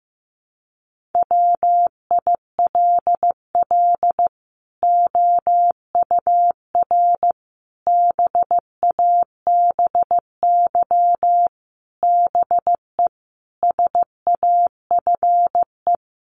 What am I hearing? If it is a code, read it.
Morse code, heard as a keyed tone, about 15 wpm, 700 Hz. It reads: WILL OUR BABY BE SAFE
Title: Sentences from Top 1300 Words 15wpm